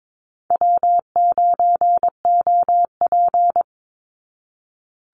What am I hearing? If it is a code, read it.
Morse code, heard as a keyed tone, about 22 wpm, 690 Hz. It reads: W9OP
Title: Call Signs Repeated 3x - 22wpm